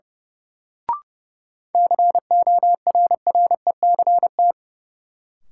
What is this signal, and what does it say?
Morse code, heard as a keyed tone, about 31 wpm, 700 Hz. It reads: CORRECT
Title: Sentences from Top 700 Words - New Word Drill - 30wpm